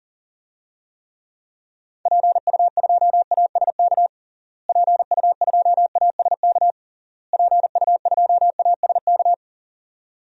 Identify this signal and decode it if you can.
Morse code, heard as a keyed tone, about 40 wpm, 700 Hz. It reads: PU2ASK PU2ASK PU2ASK
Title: DXCC-100 Call Signs - 6 Characters - Repeated 3x - 40wpm